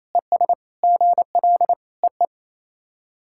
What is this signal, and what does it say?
Morse code, heard as a keyed tone, about 27 wpm, 690 Hz. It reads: ES GL EE
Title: SOTA QSO Practice 28wpm